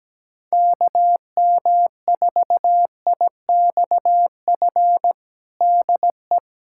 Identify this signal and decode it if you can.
Morse code, heard as a keyed tone, about 17 wpm, 700 Hz. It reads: KM4IXF DE
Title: USA Callsign DE Callsign 17wpm